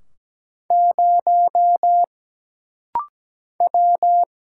Morse code, heard as a keyed tone, about 17 wpm, 700 Hz. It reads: 0 W
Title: Single Letter-Number Alt - Rapid-Fire 17wpm